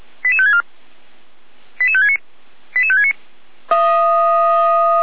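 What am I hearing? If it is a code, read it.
Five-tone selective call (ZVEI), heard as quick runs of five tones, tones of about 70 ms. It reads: 88564 88569 88569
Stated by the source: ZVEI five-tone call received on SDR, channel 455.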